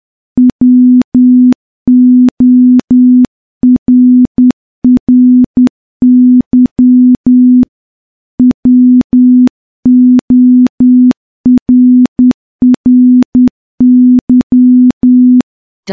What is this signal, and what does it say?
Morse code, heard as a keyed tone, about 10 wpm, 250 Hz. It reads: WORRY WORRY
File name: morse / morse-training-10WPM